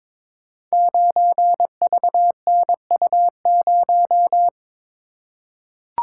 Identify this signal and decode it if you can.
Morse code, heard as a keyed tone, about 21 wpm, 700 Hz. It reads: 9VNU0
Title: Code Groups - 5 Letter:Numbers - 22wpm